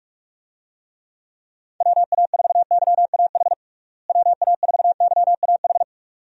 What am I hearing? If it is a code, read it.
Morse code, heard as a keyed tone, about 46 wpm, 700 Hz. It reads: WA4YAH WA4YAH
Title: USA Call Signs - 6 Characters - Repeated 2x - 45wpm